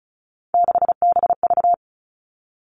Morse code, heard as a keyed tone, about 36 wpm, 700 Hz. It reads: 6BV
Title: Three Challenging Letter-Number Combinations 35wpm